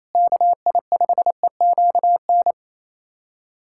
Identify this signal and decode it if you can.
Morse code, heard as a keyed tone, about 28 wpm, 700 Hz. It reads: KI5EQN
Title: USA Callsign DE Callsign - 6 Characters - 28wpm